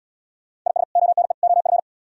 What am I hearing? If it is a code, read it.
Morse code, heard as a keyed tone, about 69 wpm, 700 Hz. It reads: IS ONE OF